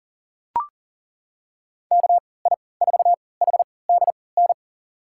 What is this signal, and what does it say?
Morse code, heard as a keyed tone, about 40 wpm, spaced as for 23 wpm, 700 Hz. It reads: KI4HDN
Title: USA Call Signs - 6 Characters - 3x Character Spacing - 40wpm